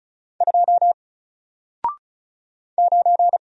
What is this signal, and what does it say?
Morse code, heard as a keyed tone, about 35 wpm, 700 Hz. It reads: 2 9
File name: Single-Letter-Number-35wpm